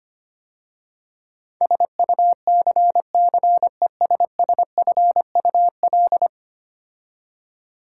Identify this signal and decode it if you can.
Morse code, heard as a keyed tone, about 25 wpm, 700 Hz. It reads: SUCCESSFUL
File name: 10 Letter Words - First 3 Letters Given - 25wpm